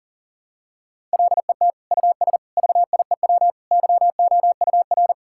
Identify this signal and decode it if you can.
Morse code, heard as a keyed tone, about 37 wpm, 700 Hz. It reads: LET US VIEW YOUR
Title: Sentences from Top 1000 Words - Review Part 6 - 40wpm